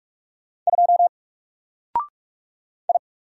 Morse code, heard as a keyed tone, about 44 wpm, 700 Hz. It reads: 2 I
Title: Single Letter-Number 45wpm